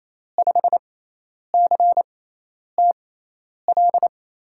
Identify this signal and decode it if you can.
Morse code, heard as a keyed tone, about 28 wpm, spaced as for 10 wpm, 700 Hz. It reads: 5CTL
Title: USA Call Signs - 6x Character Spacing - 28wpm